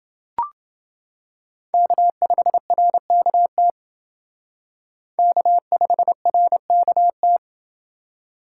Morse code, heard as a keyed tone, about 29 wpm, 700 Hz. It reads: K5RKT K5RKT